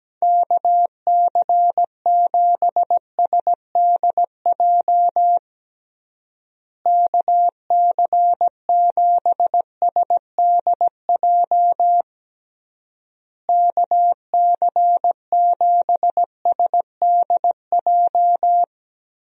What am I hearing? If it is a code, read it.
Morse code, heard as a keyed tone, about 17 wpm, 700 Hz. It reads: KC7SDJ KC7SDJ KC7SDJ
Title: USA Call Signs - 6 Characters - Repeated 3x - 17wpm